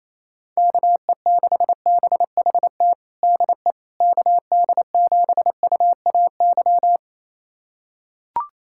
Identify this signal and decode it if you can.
Morse code, heard as a keyed tone, about 28 wpm, 700 Hz. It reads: KE6BHT DE KD7UAY
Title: USA Callsign DE Callsign 28wpm